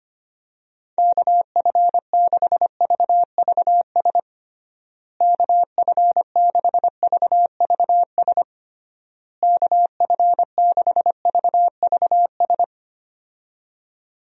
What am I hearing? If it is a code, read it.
Morse code, heard as a keyed tone, about 25 wpm, 700 Hz. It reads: KF6VVS KF6VVS KF6VVS
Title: USA Call Signs - 6 Characters - Repeated 3x - 25wpm